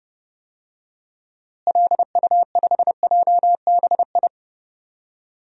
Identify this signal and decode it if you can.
Morse code, heard as a keyed tone, about 31 wpm, 700 Hz. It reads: LU5JBI
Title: DXCC-100 Call Signs - 6 Characters - Repeated 3x - 30wpm